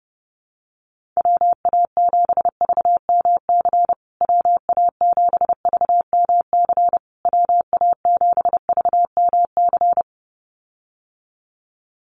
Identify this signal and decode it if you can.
Morse code, heard as a keyed tone, about 30 wpm, 700 Hz. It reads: WA7VMC WA7VMC WA7VMC
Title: USA Call Signs Repeated 3x 30wpm